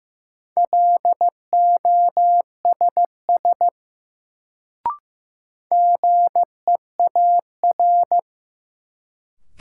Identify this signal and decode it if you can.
Morse code, heard as a keyed tone, about 15 wpm, 700 Hz. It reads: LOSS GEAR
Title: Sentences from Top 1400 Words - New Word Drill - 15wpm